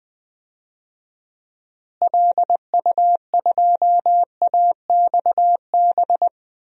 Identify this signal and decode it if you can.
Morse code, heard as a keyed tone, about 20 wpm, 700 Hz. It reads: LU2AXB
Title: DXCC-100 Call Signs - 6 Characters - Repeated 3x - 20wpm